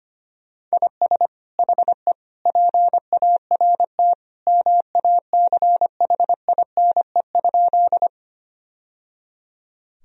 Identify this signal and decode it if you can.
Morse code, heard as a keyed tone, about 25 wpm, 700 Hz. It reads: IS HE PART MACHINE?
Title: Sentences from Top 1000 Words - Review Part 2 - 25wpm